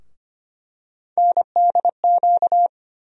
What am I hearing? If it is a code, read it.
Morse code, heard as a keyed tone, about 25 wpm, 700 Hz. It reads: NDQ
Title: USA Call Sign Postfixes - 3 Letters 25wpm